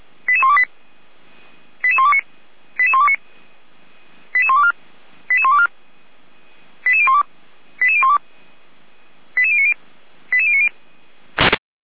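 Five-tone selective call (ZVEI), heard as quick runs of five tones, tones of about 80 ms. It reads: 88128 88129 88129 88124 88124 88012 88012 88090 88090
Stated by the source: ZVEI five-tone selective call received on SDR, fire brigade Orlamünde (feuerwehr-orlamuende).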